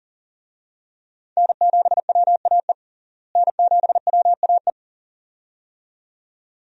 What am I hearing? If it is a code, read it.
Morse code, heard as a keyed tone, about 38 wpm, 700 Hz. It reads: N7WAE N7WAE
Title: USA Call Signs - 5 Characters - Repeated 2x - 40wpm